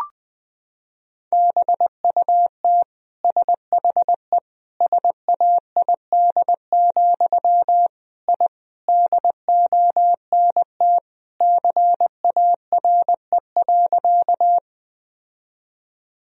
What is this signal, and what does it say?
Morse code, heard as a keyed tone, about 21 wpm, 700 Hz. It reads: BUT SHE SAID, I DONT CARE.
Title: Sentences from Top 1000 Words - Review Part 1 - 20wpm